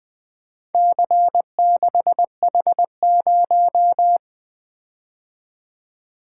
Morse code, heard as a keyed tone, about 23 wpm, 700 Hz. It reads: C6H0
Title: Speed-Racing - Four Letter-Number Combinations - 20wpm